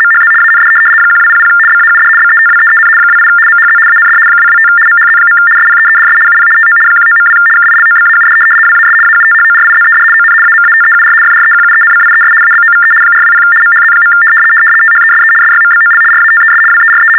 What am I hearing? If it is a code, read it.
RTTY, about 75 baud. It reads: MHZ AND 1200 BAUD BETWEEN 28 MHZ AND 225 MHZ. SPEEDS UP TO 19.2 KILOBAUD WAS AUTHORIZED ON AMATEUR FREQUENCIES ABOVE 420 MHZ. THESE SYMBOL RATES WERE
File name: teletype-RTTY-75